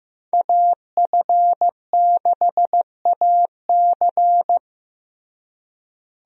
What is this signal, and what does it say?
Morse code, heard as a keyed tone, about 15 wpm, 700 Hz. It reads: AF6AC
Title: USA Call Signs - 5 Characters - Repeated 3x - 15wpm